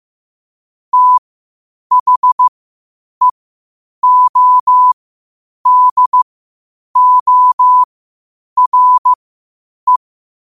Morse code, heard as a keyed tone, about 15 wpm, spaced as for 8 wpm, 1000 Hz. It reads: THEODORE
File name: qotdfiles / 8wpm